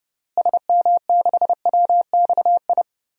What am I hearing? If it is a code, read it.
Morse code, heard as a keyed tone, about 30 wpm, 700 Hz. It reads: SM6WXI
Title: DXCC-100 Call Signs - 6 Characters - Repeated 2x - 30wpm